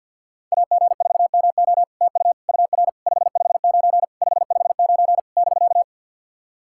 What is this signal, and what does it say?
Morse code, heard as a keyed tone, about 48 wpm, 700 Hz. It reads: AG4MO TU UR 559 559 <BK>
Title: SOTA QSO Practice 50wpm